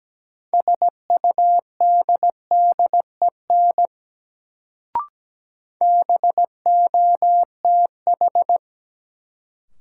Morse code, heard as a keyed tone, about 17 wpm, 700 Hz. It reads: SUDDEN BOTH
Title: Sets of 4 Words - New Word Drill - 17wpm